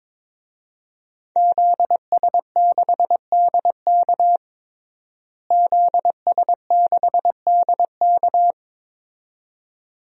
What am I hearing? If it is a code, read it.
Morse code, heard as a keyed tone, about 22 wpm, 700 Hz. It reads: ZS6DK ZS6DK
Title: DXCC-100 Call Signs - 5 Characters - Repeated 2x - 22wpm